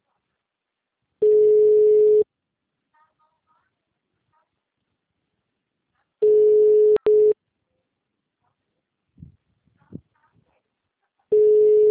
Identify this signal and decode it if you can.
Ringback tone (Europe, 425 Hz).